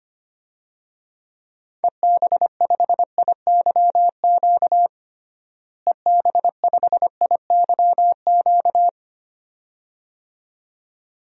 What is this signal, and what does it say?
Morse code, heard as a keyed tone, about 25 wpm, 700 Hz. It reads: EB5IYQ EB5IYQ
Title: DXCC-100 Call Signs - 6 Characters - Repeated 2x - 25wpm